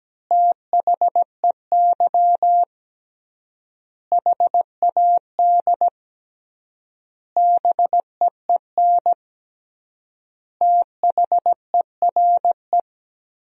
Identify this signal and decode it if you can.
Morse code, heard as a keyed tone, about 17 wpm, 700 Hz. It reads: THEY HAD BEEN THERE
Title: 3x Word-Spacing - Sentences from Top 400 Words - 17wpm